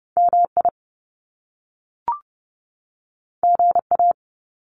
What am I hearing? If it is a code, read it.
Morse code, heard as a keyed tone, about 30 wpm, 700 Hz. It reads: MI GA
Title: US States 30wpm